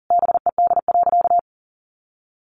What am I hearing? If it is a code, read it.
Morse code, heard as a keyed tone, about 40 wpm, 700 Hz. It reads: BED.
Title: Sentences from Top 900 Words 40wpm